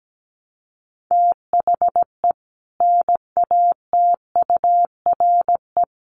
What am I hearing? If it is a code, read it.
Morse code, heard as a keyed tone, about 17 wpm, 700 Hz. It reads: THE NATURE
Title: Sets of 4 Words 17wpm